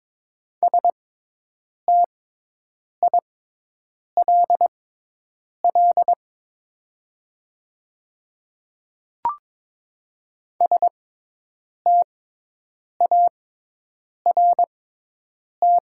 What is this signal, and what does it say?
Morse code, heard as a keyed tone, about 22 wpm, spaced as for 8 wpm, 700 Hz. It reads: STILL START
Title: 6x Character Spacing - Top 300 Words - 22wpm